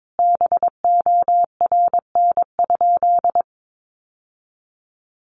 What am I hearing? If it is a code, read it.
Morse code, heard as a keyed tone, about 22 wpm, 700 Hz. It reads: BORN?
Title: Sentences from Top 1000 Words 22wpm